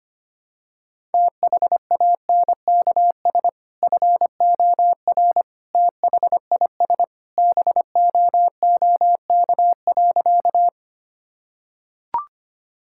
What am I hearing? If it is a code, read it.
Morse code, heard as a keyed tone, about 25 wpm, 700 Hz. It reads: THANKS FOR THIS BOOK.
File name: practice / 25wpm / Sentences with 2-Word Phrases - 25wpm